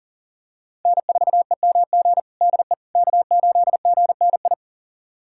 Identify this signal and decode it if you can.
Morse code, heard as a keyed tone, about 40 wpm, 700 Hz. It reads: N4EMG DE K8GNI
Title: USA Callsign DE Callsign - 5 Characters - 40wpm